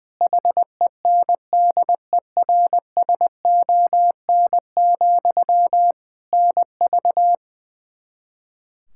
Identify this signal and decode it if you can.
Morse code, heard as a keyed tone, about 20 wpm, 700 Hz. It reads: HENDERSON, NV
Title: Practice QSO - Exchange 1 - Beginning - 20wpm